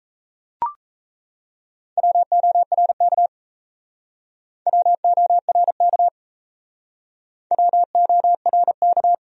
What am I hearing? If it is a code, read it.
Morse code, heard as a keyed tone, about 37 wpm, 700 Hz. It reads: WORK WORK WORK